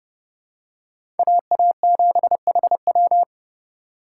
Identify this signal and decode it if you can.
Morse code, heard as a keyed tone, about 30 wpm, 700 Hz. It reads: AA7HW
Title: USA Call Signs - 5 Characters - 6x Character Spacing - 30wpm